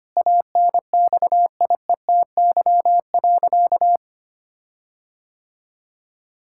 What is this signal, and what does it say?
Morse code, heard as a keyed tone, about 25 wpm, 700 Hz. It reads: ANXIETY.
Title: Sentences from Top 1600 Words 25wpm